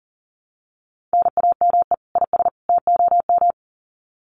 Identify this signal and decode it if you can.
Morse code, heard as a keyed tone, about 40 wpm, 700 Hz. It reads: NAME IS TOM
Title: Common QSO Phrases 40wpm